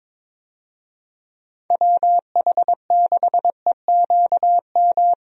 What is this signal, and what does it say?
Morse code, heard as a keyed tone, about 22 wpm, 700 Hz. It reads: WH6EQM
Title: USA Call Signs - 6 Characters - Repeated 2x - 22wpm